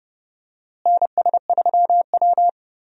Morse code, heard as a keyed tone, about 30 wpm, 700 Hz. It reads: NS3W
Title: USA Call Signs - 4 Characters - Repeated 3x - 30wpm